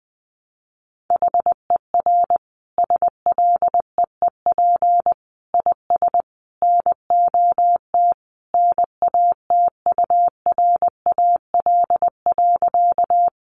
Morse code, heard as a keyed tone, about 20 wpm, 700 Hz. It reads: HER SLEEP IS NOT NATURAL.
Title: Sentences from Top 700 Words 20wpm